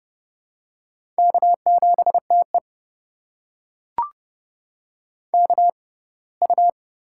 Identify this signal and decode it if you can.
Morse code, heard as a keyed tone, about 30 wpm, spaced as for 20 wpm, 700 Hz. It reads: K7TE KU
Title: USA Call Signs - 4 Characters - 6x Character Spacing - 30wpm